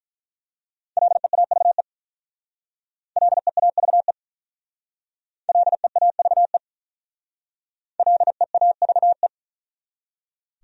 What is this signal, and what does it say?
Morse code, heard as a keyed tone, about 42 wpm, 700 Hz. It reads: LEAVE LEAVE LEAVE LEAVE
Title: Speed-Racing Top 500 Words 35wpm